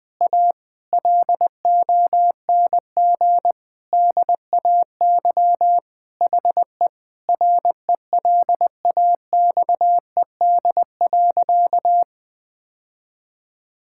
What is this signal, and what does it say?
Morse code, heard as a keyed tone, about 20 wpm, 700 Hz. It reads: A LONG DAY HE RELAXED.
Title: Sentences with 5-Word Phrases - 20wpm